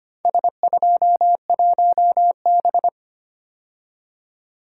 Morse code, heard as a keyed tone, about 25 wpm, 700 Hz. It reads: S21B
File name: practice / 25wpm / Four Letter-Number Combinations - 25wpm